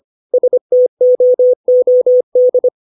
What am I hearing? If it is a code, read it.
Morse code, heard as a keyed tone, about 25 wpm, 500 Hz. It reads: STOOD
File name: 632 words / 632 Words 25WPM